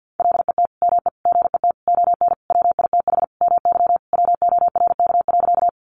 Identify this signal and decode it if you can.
Morse code, heard as a keyed tone, about 50 wpm, 700 Hz. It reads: LET ME GET ON WITH MY WORK.
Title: Sentences from Top 300 Words 50wpm